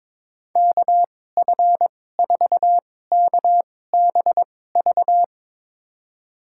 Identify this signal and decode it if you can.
Morse code, heard as a keyed tone, about 22 wpm, spaced as for 16 wpm, 700 Hz. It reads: KF4KBV
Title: USA Call Signs - 6 Characters - 2x Character Spacing - 22wpm